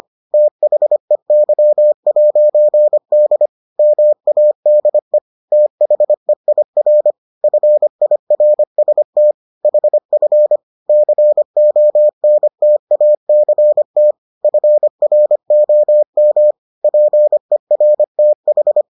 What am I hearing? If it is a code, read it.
Morse code, heard as a keyed tone, about 25 wpm, 600 Hz. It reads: THEY'D MADE THEIR FIRST HF CONTACT FROM PERTH